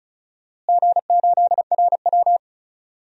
Morse code, heard as a keyed tone, about 36 wpm, 700 Hz. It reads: G8RW
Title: DXCC-100 Call Signs - 4 Characters - Repeated 3x - 35wpm